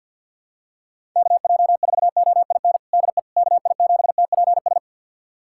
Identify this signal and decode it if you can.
Morse code, heard as a keyed tone, about 50 wpm, 700 Hz. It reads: KJ4OIN DE KI7TPS
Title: USA Callsign DE Callsign 50wpm